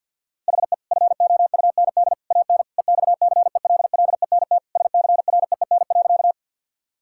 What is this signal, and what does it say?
Morse code, heard as a keyed tone, about 50 wpm, 700 Hz. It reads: HE FOUND AN EXCELLENT SCREEN.
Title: Sentences from Top 1000 Words - Review Part 6 - 50wpm